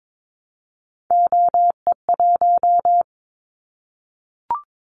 Morse code, heard as a keyed tone, about 22 wpm, 700 Hz. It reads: OE1
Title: Callsign Prefixes for Top 100 DXCC - 22wpm